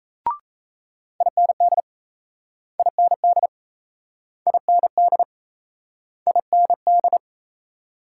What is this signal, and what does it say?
Morse code, heard as a keyed tone, about 38 wpm, 700 Hz. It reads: IND IND IND IND